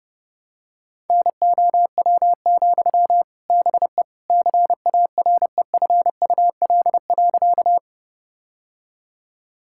Morse code, heard as a keyed tone, about 30 wpm, 700 Hz. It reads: NOW, BE CAREFUL.
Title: Sentences from Top 1600 Words 30wpm